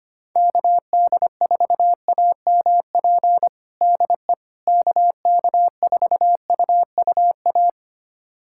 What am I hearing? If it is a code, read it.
Morse code, heard as a keyed tone, about 25 wpm, 700 Hz. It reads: KD4AMP DE KK4UUA
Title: USA Callsign DE Callsign 25wpm